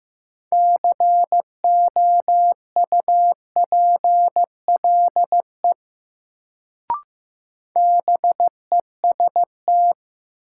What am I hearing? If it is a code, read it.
Morse code, heard as a keyed tone, about 15 wpm, 700 Hz. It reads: COUPLE BEST
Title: Sets of 2 Words - New Word Drill - 15wpm